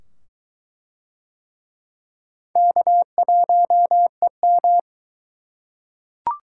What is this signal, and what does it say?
Morse code, heard as a keyed tone, about 22 wpm, 700 Hz. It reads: K1EM